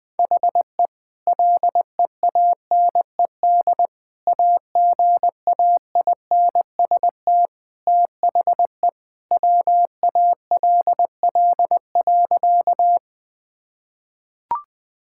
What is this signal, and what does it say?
Morse code, heard as a keyed tone, about 20 wpm, 700 Hz. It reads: HE LEANED AGAINST THE WALL.
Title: Sentences with 2-Word Phrases - 20wpm